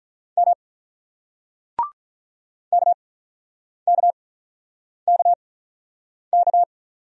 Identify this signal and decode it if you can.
Morse code, heard as a keyed tone, about 46 wpm, spaced as for 9 wpm, 700 Hz. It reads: M KKKK